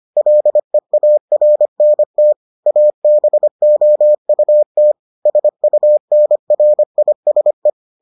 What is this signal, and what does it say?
Morse code, heard as a keyed tone, about 25 wpm, 600 Hz. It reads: LEARNT ABOUT SUNRISE